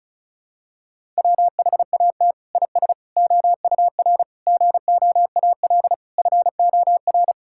Morse code, heard as a keyed tone, about 34 wpm, 700 Hz. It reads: WHAT IS OUR GOAL FOR
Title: Sentences from Top 1000 Words - Review Part 5 - 35wpm